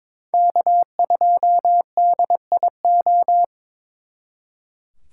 Morse code, heard as a keyed tone, about 26 wpm, 700 Hz. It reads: K2DIO
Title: Speed Racing - USA 1x3 Call Signs 22wpm